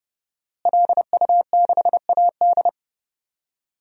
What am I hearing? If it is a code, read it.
Morse code, heard as a keyed tone, about 30 wpm, 700 Hz. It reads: LU6AD
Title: DXCC-100 Call Signs - 5 Characters - Repeated 3x - 30wpm